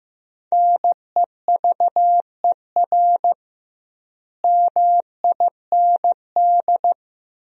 Morse code, heard as a keyed tone, about 15 wpm, 700 Hz. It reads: NEVER MIND
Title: 2x Word-Spacing - Sentences from Top 500 Words - 15wpm